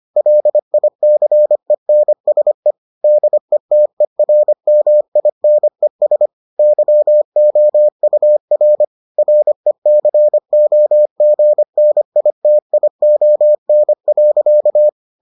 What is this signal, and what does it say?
Morse code, heard as a keyed tone, about 25 wpm, 600 Hz. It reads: LICENSE DETERMINES YOUR RECOGNITION.